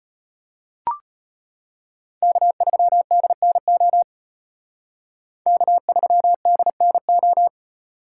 Morse code, heard as a keyed tone, about 33 wpm, 700 Hz. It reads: K3DNO K3DNO